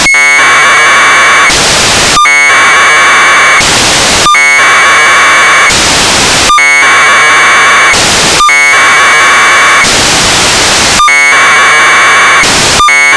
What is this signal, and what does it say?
Packet radio (AX.25, as in APRS).